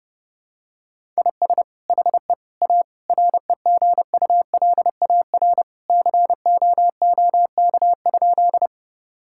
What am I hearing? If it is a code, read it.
Morse code, heard as a keyed tone, about 30 wpm, 700 Hz. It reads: IS HE A REGULAR COOK?
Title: Sentences from Top 1000 Words - Review Part 6 - 30wpm